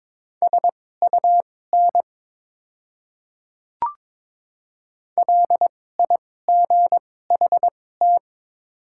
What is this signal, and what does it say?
Morse code, heard as a keyed tone, about 22 wpm, spaced as for 16 wpm, 700 Hz. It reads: SUN LIGHT
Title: 2x Character Spacing - Top 500 Words - 22wpm